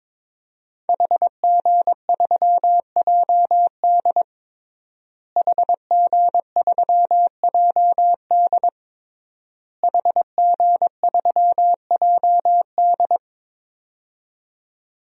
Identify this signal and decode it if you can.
Morse code, heard as a keyed tone, about 22 wpm, 700 Hz. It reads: HG3JD HG3JD HG3JD
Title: DXCC-100 Call Signs - 5 Characters - Repeated 3x - 22wpm